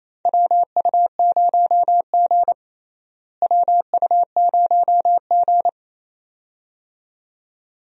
Morse code, heard as a keyed tone, about 28 wpm, 700 Hz. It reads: WU0G WU0G
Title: USA Call Signs - 4 Characters - Repeated 2x - 28wpm